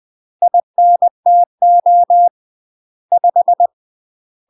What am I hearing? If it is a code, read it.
Morse code, heard as a keyed tone, about 20 wpm, 700 Hz. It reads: INTO 5
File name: UK MASTER QSOs TRAINING / 20WPM / UK MASTER QSOs 0017